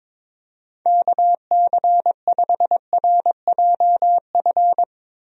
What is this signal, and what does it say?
Morse code, heard as a keyed tone, about 22 wpm, 700 Hz. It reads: KC5RJF
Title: USA Call Signs - 6 Characters - Repeated 2x - 22wpm